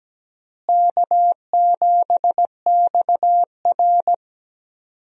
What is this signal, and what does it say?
Morse code, heard as a keyed tone, about 17 wpm, 700 Hz. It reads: K7XR
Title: USA Call Signs - 4 Characters - Repeated 2x - 17wpm